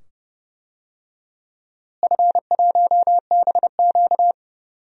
Morse code, heard as a keyed tone, about 30 wpm, 700 Hz. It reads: F1BQ
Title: DXCC-100 Call Signs - 4 Characters - Repeated 2x - 30wpm